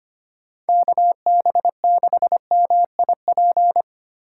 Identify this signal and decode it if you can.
Morse code, heard as a keyed tone, about 25 wpm, 700 Hz. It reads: KB6MIP
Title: USA Call Signs - 2x Character Spacing - 25wpm